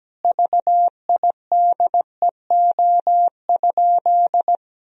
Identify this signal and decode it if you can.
Morse code, heard as a keyed tone, about 17 wpm, 700 Hz. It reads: VIDEO?
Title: Sentences from Top 1000 Words - Review Part 4 - 17wpm